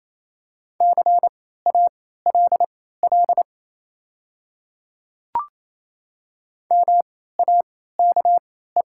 Morse code, heard as a keyed tone, about 28 wpm, spaced as for 16 wpm, 690 Hz. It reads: CALL MAKE
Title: 3x Character Spacing - Top 100 Words - 28wpm